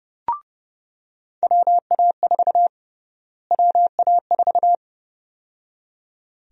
Morse code, heard as a keyed tone, about 30 wpm, 700 Hz. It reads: WA4 WA4
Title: USA Call Sign Prefixes - Repeated 2x - 30wpm